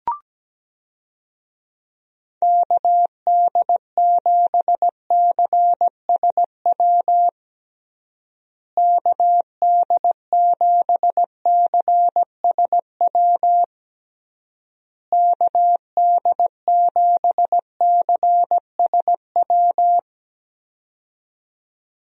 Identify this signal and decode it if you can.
Morse code, heard as a keyed tone, about 17 wpm, 700 Hz. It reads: KD7CSW KD7CSW KD7CSW
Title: USA Call Signs - 6 Characters - Repeated 3x - 17wpm